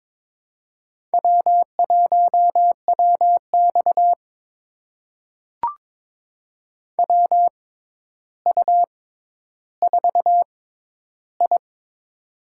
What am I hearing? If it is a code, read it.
Morse code, heard as a keyed tone, about 22 wpm, spaced as for 11 wpm, 700 Hz. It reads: W1WX WU4I
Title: USA Call Signs - 4 Characters - 6x Character Spacing - 22wpm